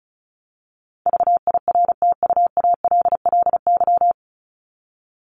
Morse code, heard as a keyed tone, about 34 wpm, 700 Hz. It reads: VIRTUALLY
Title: LY Suffix - Top 100 Words with Base Words 35wpm